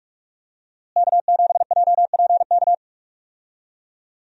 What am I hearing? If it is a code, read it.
Morse code, heard as a keyed tone, about 43 wpm, 700 Hz. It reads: K7JPK
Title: USA Call Signs - 5 Characters - Repeated 2x - 45wpm